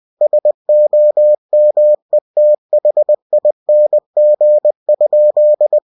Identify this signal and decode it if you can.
Morse code, heard as a keyed tone, about 20 wpm, 600 Hz. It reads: SOMETHING?
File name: cw-books / winnie-the-pooh / winnie-20wpm-page-0083